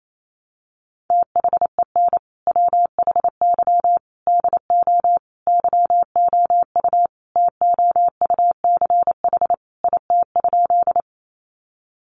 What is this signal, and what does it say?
Morse code, heard as a keyed tone, about 28 wpm, 700 Hz. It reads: THEN WHY DO YOU TOUCH IT?
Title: Sentences from Top 1200 Words 28wpm